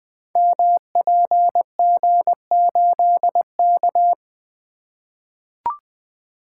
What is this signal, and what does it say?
Morse code, heard as a keyed tone, about 20 wpm, 700 Hz. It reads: MPG8K
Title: Code Groups - 5 Letter:Numbers - 20wpm